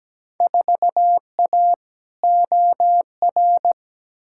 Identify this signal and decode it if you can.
Morse code, heard as a keyed tone, about 17 wpm, 700 Hz. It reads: 4A OR
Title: Field Day Exchanges - 17wpm